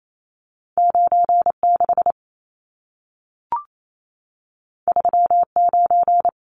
Morse code, heard as a keyed tone, about 28 wpm, 700 Hz. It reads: 96 39
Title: Numbers - 2-Digit 28wpm